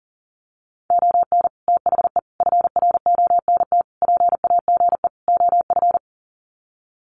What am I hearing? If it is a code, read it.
Morse code, heard as a keyed tone, about 40 wpm, 700 Hz. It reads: ON THE FRONT PAGE OF
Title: Sets of 5 Words Encore 40wpm